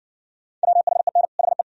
Morse code, heard as a keyed tone, about 60 wpm, 700 Hz. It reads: WHEN HE